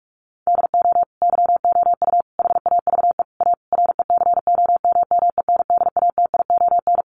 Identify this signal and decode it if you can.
Morse code, heard as a keyed tone, about 44 wpm, 700 Hz. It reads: DO YOU HAVE A RECOMMENDATION
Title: Sentences from Top 1400 Words 45wpm